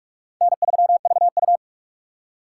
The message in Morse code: N3VU

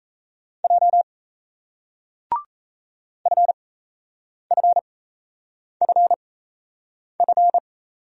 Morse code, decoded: J FFFF

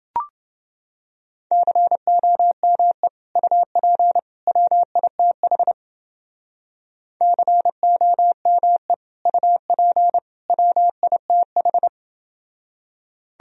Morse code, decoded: COME UP WITH COME UP WITH

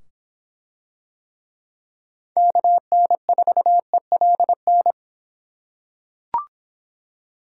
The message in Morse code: KN4ELN

